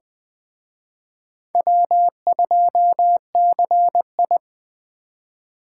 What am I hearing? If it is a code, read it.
Morse code, heard as a keyed tone, about 20 wpm, 700 Hz. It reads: W2CI